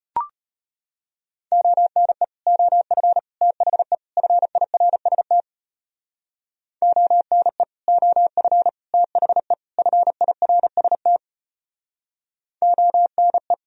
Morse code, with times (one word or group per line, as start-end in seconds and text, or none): 1.5-2.2 ONE
2.5-3.2 OF
3.4-4.0 THE
4.2-5.4 FIRST
6.8-7.6 ONE
7.9-8.7 OF
8.9-9.5 THE
9.8-11.2 FIRST
12.6-13.5 ONE